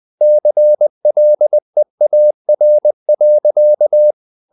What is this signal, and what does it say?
Morse code, heard as a keyed tone, about 20 wpm, 600 Hz. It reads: CLEAR.
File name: the old man and the sea part 2 20wpm